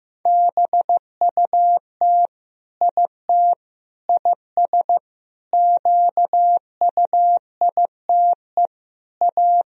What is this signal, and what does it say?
Morse code, heard as a keyed tone, about 15 wpm, 700 Hz. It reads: BUT IT IS QUITE A
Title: Sentences from Top 1000 Words - Review Part 2 - 15wpm